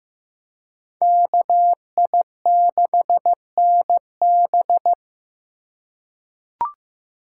Morse code, heard as a keyed tone, about 15 wpm, 700 Hz. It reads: KI6NB
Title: USA Call Signs - 5 Characters - 4x Character Spacing - 15wpm